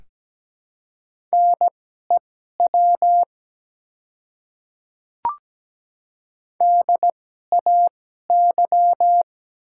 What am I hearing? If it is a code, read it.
Morse code, heard as a keyed tone, about 17 wpm, spaced as for 12 wpm, 700 Hz. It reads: NEW DAY
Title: 2x Character Spacing - Top 300 Words - 17wpm